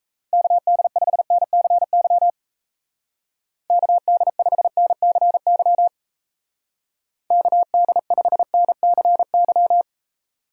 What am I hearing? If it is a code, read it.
Morse code, heard as a keyed tone, about 33 wpm, 700 Hz. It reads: KD5NCY KD5NCY KD5NCY
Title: Speed Racing - USA Call Signs 28wpm